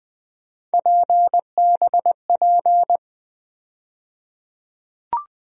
Morse code, heard as a keyed tone, about 20 wpm, 700 Hz. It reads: PBP
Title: USA Call Sign Postfixes - 3 Letters 20wpm